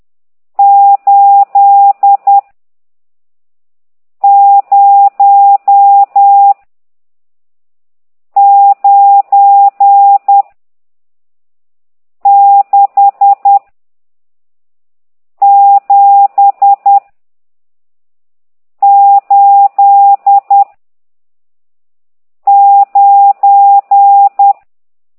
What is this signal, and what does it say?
Morse code, heard as a keyed tone, about 10 wpm, 800 Hz. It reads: 8 0 9 6 7 8 9
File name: ct2idw / exerc14